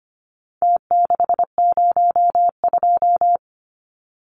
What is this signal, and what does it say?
Morse code, heard as a keyed tone, about 25 wpm, 700 Hz. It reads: T602
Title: Numbers - Contest Style - 4-Digit 25wpm